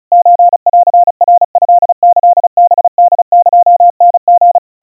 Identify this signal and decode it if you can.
Morse code (Japanese Wabun code), heard as a keyed tone, about 34 wpm, 700 Hz. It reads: ソンナトキハホエタリ
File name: kawa / Morse sample / kamyu 130 cpm 700 Hz Crispy